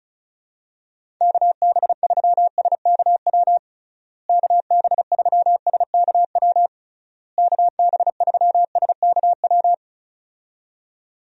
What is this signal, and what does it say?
Morse code, heard as a keyed tone, about 36 wpm, 700 Hz. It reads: KB3SKW KB3SKW KB3SKW
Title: USA Call Signs - 6 Characters - Repeated 3x - 35wpm